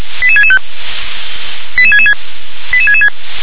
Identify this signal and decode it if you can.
Five-tone selective call (ZVEI), heard as quick runs of five tones, tones of about 70 ms. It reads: 88665 88666 88666